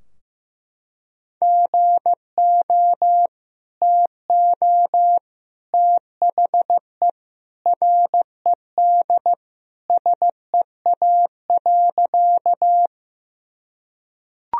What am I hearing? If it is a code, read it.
Morse code, heard as a keyed tone, about 16 wpm, 700 Hz. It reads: GO TO THE RED SEA.
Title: Sentences from Top 1000 Words - Review Part 2 - 15wpm